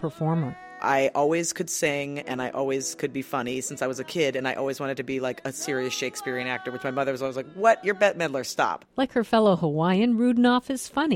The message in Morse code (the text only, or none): none